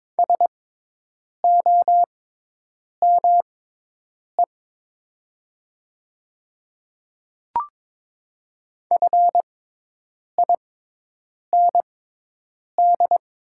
Morse code, heard as a keyed tone, about 22 wpm, spaced as for 8 wpm, 700 Hz. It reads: SOME FIND
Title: 6x Character Spacing - Top 100 Words - 22wpm